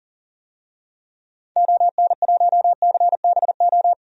Morse code, heard as a keyed tone, about 40 wpm, 700 Hz. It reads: ON1CBO
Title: DXCC-100 Call Signs - 6 Characters - 40wpm